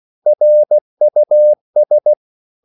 Morse code, heard as a keyed tone, about 16 wpm, 600 Hz. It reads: RUS
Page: Korte danske ord 16wpm